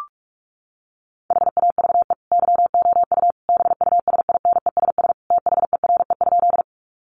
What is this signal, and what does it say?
Morse code, heard as a keyed tone, about 44 wpm, 700 Hz. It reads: HAVE YOU BUSINESS THERE?
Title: Sentences from Top 600 Words 45wpm